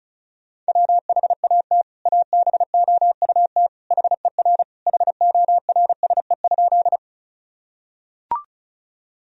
Morse code, heard as a keyed tone, about 34 wpm, 700 Hz. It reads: WHAT ABOUT HER HORSE?
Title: Sentences from Top 1000 Words - Review Part 1 - 35wpm